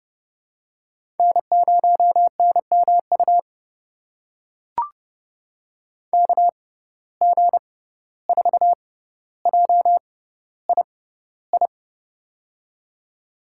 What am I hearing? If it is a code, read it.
Morse code, heard as a keyed tone, about 29 wpm, spaced as for 14 wpm, 700 Hz. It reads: N0NMU KG4JII
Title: USA Call Signs - 6x Character Spacing - 30wpm